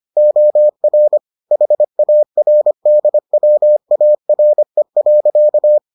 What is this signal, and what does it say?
Morse code, heard as a keyed tone, about 25 wpm, 600 Hz. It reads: OR HARDWARE.